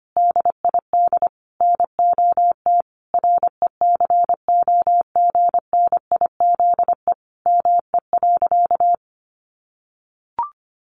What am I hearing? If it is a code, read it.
Morse code, heard as a keyed tone, about 25 wpm, 700 Hz. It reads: DID NOT RECOGNIZE ME.
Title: Sentences from Top 1400 Words 25wpm